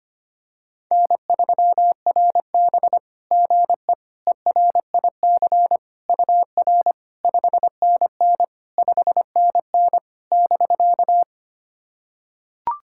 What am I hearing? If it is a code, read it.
Morse code, heard as a keyed tone, about 25 wpm, 700 Hz. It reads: N3RB GE ERIC UR 5NN 5NN <BK>
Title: SOTA QSO Practice 25wpm